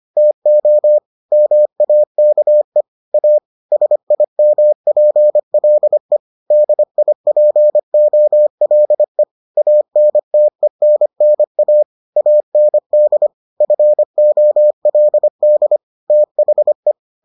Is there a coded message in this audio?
Yes: Morse code from 0.2 to 16.9 s, heard as a keyed tone, about 25 wpm, 600 Hz. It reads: TO MAKE A SIMPLE DIPOLE ANTENNA AND FOLD THE